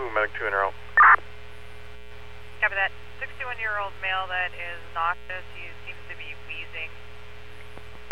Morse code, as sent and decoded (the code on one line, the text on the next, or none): none